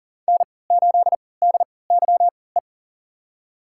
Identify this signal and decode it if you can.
Morse code, heard as a keyed tone, about 39 wpm, spaced as for 22 wpm, 700 Hz. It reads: N8DYE
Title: USA Call Signs - 5 Characters - 3x Character Spacing - 40wpm